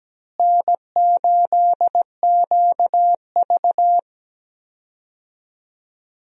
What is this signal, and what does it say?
Morse code, heard as a keyed tone, about 17 wpm, 700 Hz. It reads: N8QV
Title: USA Call Signs - 4 Characters - Repeated 2x - 17wpm